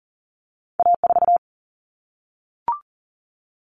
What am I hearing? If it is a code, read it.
Morse code, heard as a keyed tone, about 40 wpm, 700 Hz. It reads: A4
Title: Common Cut Numbers - 2-Digit 40wpm